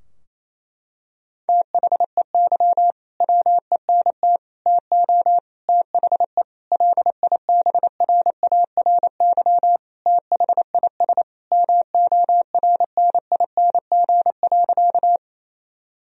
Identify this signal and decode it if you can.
Morse code, heard as a keyed tone, about 28 wpm, 700 Hz. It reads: THEY WENT TO THE LIBRARY THIS MORNING.
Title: Sentences from Top 1000 Words - Review Part 4 - 28wpm